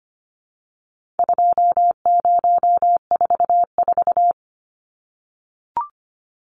Morse code, heard as a keyed tone, about 25 wpm, 700 Hz. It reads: 2044
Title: Numbers - 4-Digit 25wpm